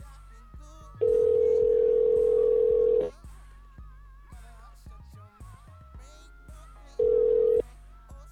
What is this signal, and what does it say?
Ringback tone (North America), 440 + 480 Hz.